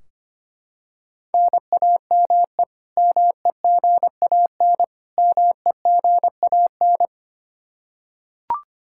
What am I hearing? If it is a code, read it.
Morse code, heard as a keyed tone, about 25 wpm, 700 Hz. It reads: NAME MEGAN MEGAN
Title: Practice QSO - Exchange 1 - Beginning - 25wpm